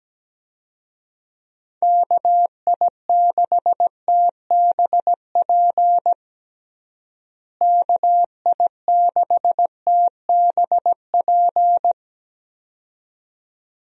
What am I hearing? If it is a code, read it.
Morse code, heard as a keyed tone, about 17 wpm, 700 Hz. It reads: KI6TBP KI6TBP
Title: USA Call Signs - 6 Characters - Repeated 2x - 17wpm